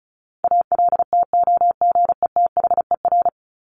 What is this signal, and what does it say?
Morse code, heard as a keyed tone, about 34 wpm, 700 Hz. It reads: ALTOGETHER
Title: Top 1501-1750 Words 35wpm